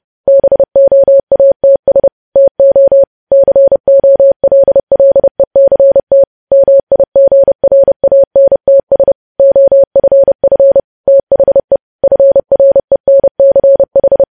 Morse code, heard as a keyed tone, about 30 wpm, 550 Hz. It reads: BOATS TO COLLECT MIGRANTS OFF THE FRENCH
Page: news headlines in morse code at 30 wpm